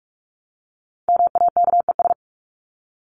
Morse code, heard as a keyed tone, about 44 wpm, 700 Hz. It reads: MAKES
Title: ES:S Suffix - Top 100 Words with Base Words 45wpm